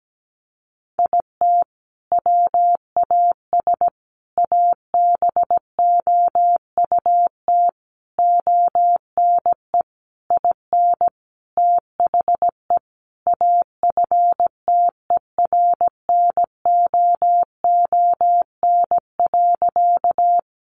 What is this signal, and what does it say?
Morse code, heard as a keyed tone, about 17 wpm, 700 Hz. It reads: IT WAS ABOUT ONE IN THE AFTERNOON.
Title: Sentences from Top 1500 Words 17wpm